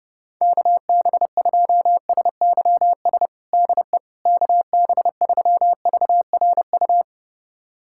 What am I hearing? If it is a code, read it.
Morse code, heard as a keyed tone, about 30 wpm, 700 Hz. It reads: KB2SYS DE KB3VRU
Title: USA Callsign DE Callsign - 6 Characters - 30wpm